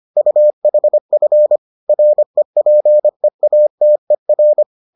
Morse code, heard as a keyed tone, about 25 wpm, 600 Hz. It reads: UHF REPEATER